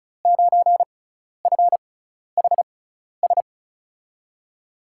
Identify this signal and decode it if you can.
Morse code, heard as a keyed tone, about 36 wpm, spaced as for 12 wpm, 700 Hz. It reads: 9FHS
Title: USA Call Signs - 5 Characters - 6x Character Spacing - 35wpm